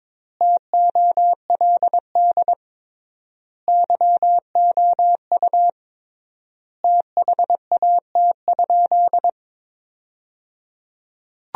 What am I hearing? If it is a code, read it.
Morse code, heard as a keyed tone, about 21 wpm, 700 Hz. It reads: TOLD YOU THAT?
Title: 3x Word-Spacing - Sentences from Top 400 Words - 22wpm